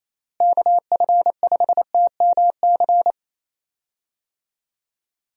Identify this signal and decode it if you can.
Morse code, heard as a keyed tone, about 27 wpm, 700 Hz. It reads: KF5TMC
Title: USA Call Signs - 6 Characters - Repeated 3x - 28wpm